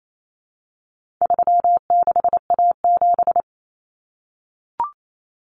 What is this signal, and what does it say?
Morse code, heard as a keyed tone, about 28 wpm, 700 Hz. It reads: 36A7
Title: Common Cut Numbers - 4-Digit 28wpm